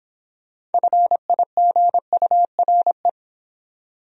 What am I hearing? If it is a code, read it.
Morse code, heard as a keyed tone, about 31 wpm, 700 Hz. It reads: FIGURE